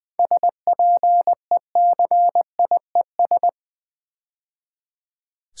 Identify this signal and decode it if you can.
Morse code, heard as a keyed tone, about 20 wpm, 700 Hz. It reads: SPECIES
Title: 7 Letter Words - First 3 Letters Given - 20wpm